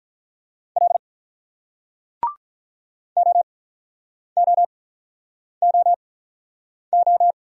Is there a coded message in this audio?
Yes: Morse code from 0.8 to 7.3 s, heard as a keyed tone, about 46 wpm, spaced as for 9 wpm, 700 Hz. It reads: L OOOO